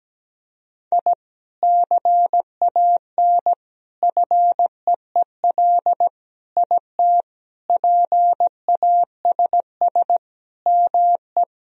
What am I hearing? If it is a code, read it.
Morse code, heard as a keyed tone, about 17 wpm, 700 Hz. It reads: I CAN FEEL IT PASS ME
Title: Sentences from Top 1000 Words - Review Part 2 - 17wpm